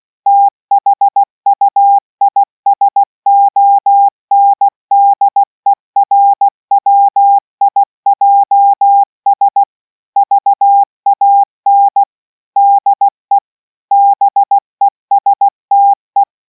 Morse code, heard as a keyed tone, about 16 wpm, 800 Hz. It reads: THUISONDERWIJS VAN DE BESTE